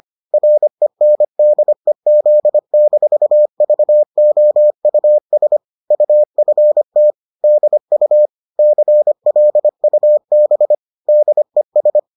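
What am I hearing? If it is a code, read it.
Morse code, heard as a keyed tone, about 25 wpm, 600 Hz. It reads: RENDEZ-VOUS UFT DU CLUB DES
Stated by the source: Bulletin sent in CW.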